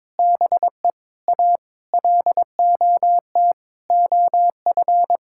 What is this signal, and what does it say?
Morse code, heard as a keyed tone, about 21 wpm, 700 Hz. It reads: BE A LOT OF
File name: Sentences with 7-Word Phrases - 22wpm